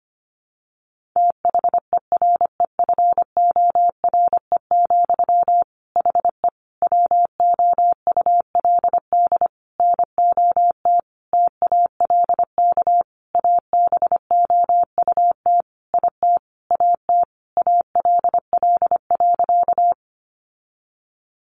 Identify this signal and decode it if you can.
Morse code, heard as a keyed tone, about 25 wpm, 700 Hz. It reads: THEREFORE, HE WOULD NOT TALK ABOUT IT AT ALL.
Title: Sentences from Top 800 Words 25wpm